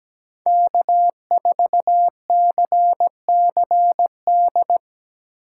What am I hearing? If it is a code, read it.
Morse code, heard as a keyed tone, about 17 wpm, 700 Hz. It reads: K4CCD
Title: USA Call Signs - 5 Characters - 3x Character Spacing - 17wpm